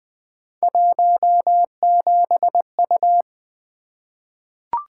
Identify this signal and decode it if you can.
Morse code, heard as a keyed tone, about 20 wpm, 700 Hz. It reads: 17U
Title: Three Letter-Number Combinations - 20wpm